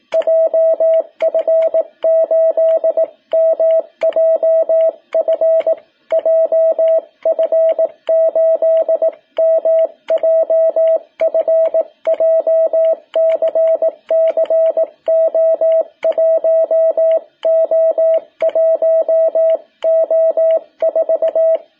Morse code, heard as a keyed tone, about 18 wpm, 660 Hz. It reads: JF8MJF JF8MJF JCC O1O1O4
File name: jh8cbh / 2021cw